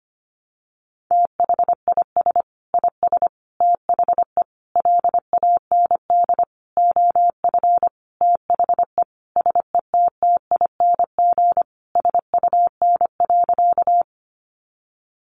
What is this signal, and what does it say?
Morse code, heard as a keyed tone, about 25 wpm, 700 Hz. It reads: THIS IS THE LAND OF THE SETTING SUN.
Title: Sentences from Top 1300 Words 25wpm